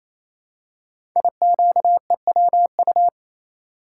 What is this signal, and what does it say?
Morse code, heard as a keyed tone, about 27 wpm, 700 Hz. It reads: IQEWU
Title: Code Groups - 5 Letters - 28wpm